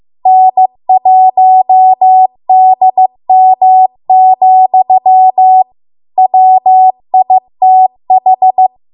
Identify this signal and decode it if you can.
Morse code, heard as a keyed tone, about 15 wpm, 750 Hz. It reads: N1DM, WITH